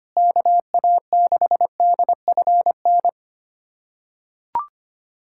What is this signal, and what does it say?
Morse code, heard as a keyed tone, about 25 wpm, 700 Hz. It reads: KA6DFN
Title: USA Callsign DE Callsign 25wpm